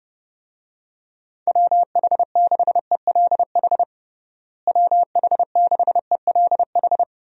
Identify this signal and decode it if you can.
Morse code, heard as a keyed tone, about 30 wpm, 700 Hz. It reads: WH6ELH WH6ELH
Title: USA Call Signs - 6 Characters - Repeated 2x - 30wpm